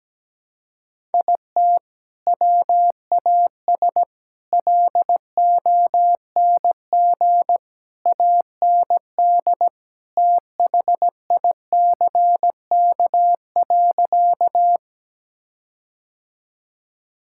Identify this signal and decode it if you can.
Morse code, heard as a keyed tone, about 17 wpm, 700 Hz. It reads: IT WAS LONG AND THICK.
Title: Sentences from Top 1600 Words 17wpm